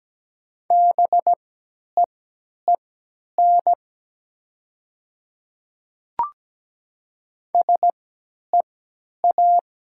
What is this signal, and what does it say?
Morse code, heard as a keyed tone, about 17 wpm, spaced as for 10 wpm, 700 Hz. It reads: BEEN SEA
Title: 3x Character Spacing - Top 300 Words - 17wpm